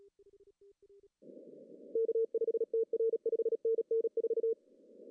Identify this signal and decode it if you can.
Morse code, heard as a keyed tone, about 36 wpm, 470 Hz. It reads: K5TR 5NN 4
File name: CW / cqww-cw-k5tr